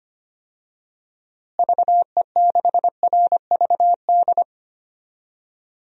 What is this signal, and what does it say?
Morse code, heard as a keyed tone, about 25 wpm, 700 Hz. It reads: VE6RVD
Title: DXCC-100 Call Signs - 6 Characters - Repeated 3x - 25wpm